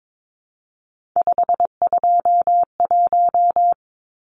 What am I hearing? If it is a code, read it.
Morse code, heard as a keyed tone, about 22 wpm, 700 Hz. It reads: 521
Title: Numbers - 3-Digit 22wpm